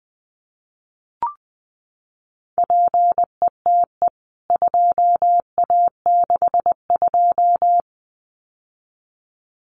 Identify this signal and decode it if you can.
Morse code, heard as a keyed tone, about 20 wpm, 700 Hz. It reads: PETE 2A62
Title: CWT Exchange 20wpm